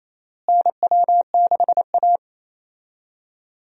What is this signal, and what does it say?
Morse code, heard as a keyed tone, about 28 wpm, 700 Hz. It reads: NW6A
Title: USA Call Signs - 4 Characters - Repeated 2x - 28wpm